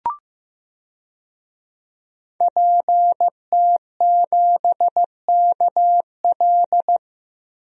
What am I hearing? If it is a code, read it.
Morse code, heard as a keyed tone, about 15 wpm, 700 Hz. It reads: PT7KL
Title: DXCC-100 Call Signs - 5 Characters - 15wpm